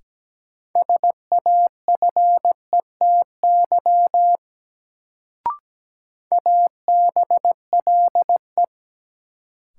Morse code, heard as a keyed tone, about 17 wpm, 700 Hz. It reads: SAFETY ABLE
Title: Sentences from Top 700 Words - New Word Drill - 17wpm